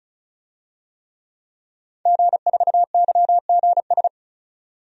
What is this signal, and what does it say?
Morse code, heard as a keyed tone, about 36 wpm, 700 Hz. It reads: G4YGS
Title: DXCC-100 Call Signs - 5 Characters - 35wpm